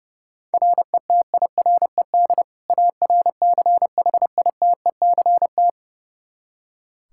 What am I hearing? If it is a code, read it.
Morse code, heard as a keyed tone, about 30 wpm, 700 Hz. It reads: RETIRED ARCHITECT
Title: Practice QSO - Exchange 3 - More Info - 30wpm